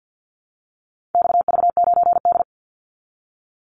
Morse code, heard as a keyed tone, about 50 wpm, 700 Hz. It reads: XV9D
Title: Call Signs Repeated 3x - 50wpm